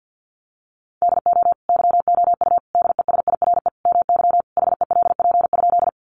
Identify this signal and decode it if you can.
Morse code, heard as a keyed tone, about 50 wpm, 700 Hz. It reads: DO YOU DESIRE MY HELP?
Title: Sentences from Top 1100 Words 50wpm